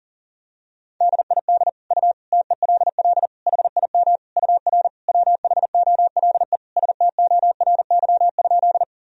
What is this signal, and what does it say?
Morse code, heard as a keyed tone, about 40 wpm, 700 Hz. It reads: DID U TELL HIM UR WHOLE STORY?
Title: Sentences from Top 400 Words and Abbreviations 40wpm